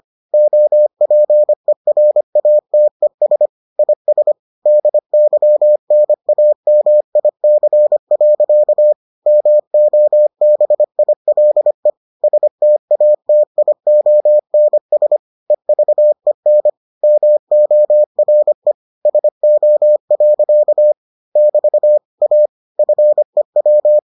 Morse code, heard as a keyed tone, about 25 wpm, 600 Hz. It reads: OPERATES IS DYNAMIC. MOBILE STATIONS EVEN MORE SO. <BT> A FEW